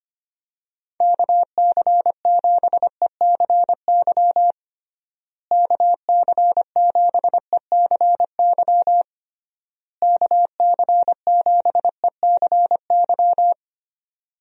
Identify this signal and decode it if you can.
Morse code, heard as a keyed tone, about 25 wpm, 700 Hz. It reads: KC7ECY KC7ECY KC7ECY
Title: USA Call Signs - 6 Characters - Repeated 3x - 25wpm